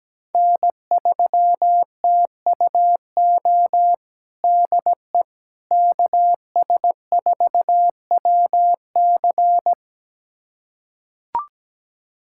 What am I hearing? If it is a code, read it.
Morse code, heard as a keyed tone, about 17 wpm, 700 Hz. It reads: N3TUO DE KS4WC
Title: USA Callsign DE Callsign - 5 Characters - 17wpm